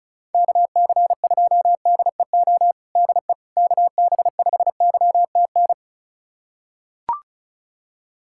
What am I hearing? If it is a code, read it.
Morse code, heard as a keyed tone, about 36 wpm, 700 Hz. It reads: KC2DEO DE KB5YTN
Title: USA Callsign DE Callsign 35wpm